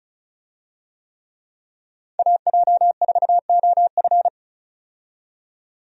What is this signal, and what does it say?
Morse code, heard as a keyed tone, about 36 wpm, 700 Hz. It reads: AJ4OF
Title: USA Call Signs - 5 Characters - 35wpm